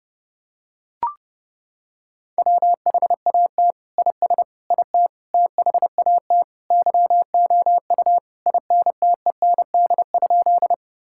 Morse code, heard as a keyed tone, about 32 wpm, 700 Hz. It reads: WHAT IS IT THAT YOU INTEND?
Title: Sentences from Top 1600 Words 30wpm